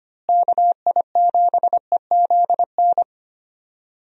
Morse code, heard as a keyed tone, about 25 wpm, 700 Hz. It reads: KI7EZN
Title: USA Call Signs - 6 Characters - Repeated 2x - 25wpm